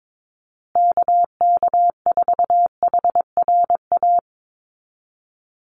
Morse code, heard as a keyed tone, about 22 wpm, 700 Hz. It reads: KK4HRA
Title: USA Call Signs Repeated 2x 22wpm